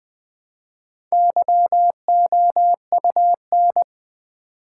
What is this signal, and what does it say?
Morse code, heard as a keyed tone, about 20 wpm, 700 Hz. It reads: YOUN
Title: Build A Word - 1 Letter At A Time - 3 to 5-Letter Words - 20wpm